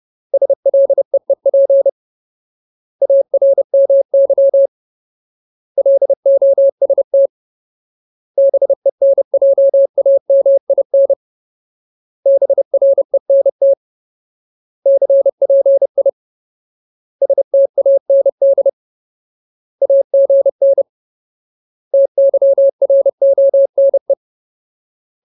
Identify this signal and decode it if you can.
Morse code, heard as a keyed tone, about 30 wpm, 550 Hz. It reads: SLEEP ARMY LOST BENJAMIN BRENT CPI STAND AGN TYRONE
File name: Ultimate Word List / 30WPM Ultimate